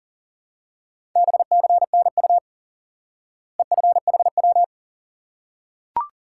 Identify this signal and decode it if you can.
Morse code, heard as a keyed tone, about 40 wpm, 700 Hz. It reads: BCNU EFHW
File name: IQE Course 3 - Lesson 89 - YAGI - Two Elements - 40wpm